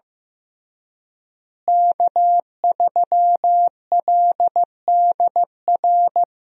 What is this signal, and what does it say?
Morse code, heard as a keyed tone, about 15 wpm, 700 Hz. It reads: K3LDR
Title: USA Call Signs - 5 Characters - Repeated 3x - 15wpm